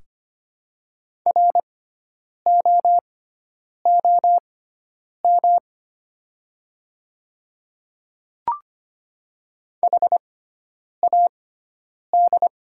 Morse code, heard as a keyed tone, about 25 wpm, spaced as for 9 wpm, 700 Hz. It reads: ROOM HAD
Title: 6x Character Spacing - Top 300 Words - 25wpm